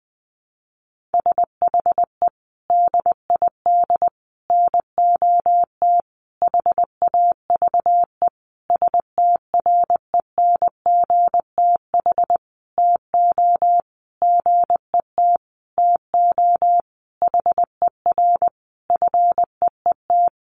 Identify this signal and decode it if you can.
Morse code, heard as a keyed tone, about 20 wpm, 700 Hz. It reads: SHE DID NOT HAVE STRENGTH TO GET TO HER FEET
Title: Sentences from Top 1300 Words 20wpm